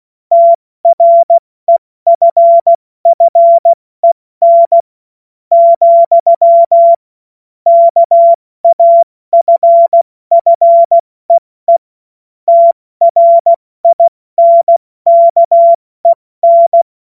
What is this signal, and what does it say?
Morse code, heard as a keyed tone, about 16 wpm, spaced as for 14 wpm, 680 Hz. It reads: TREFFEN, KAFFEE TRINKEN
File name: Morsekurs / Witze / Witz 24.Oktober 2023 70 F80